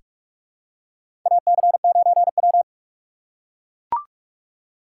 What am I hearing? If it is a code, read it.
Morse code, heard as a keyed tone, about 42 wpm, 700 Hz. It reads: AC9W
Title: USA Call Signs - 4 Characters - 6x Character Spacing - 45wpm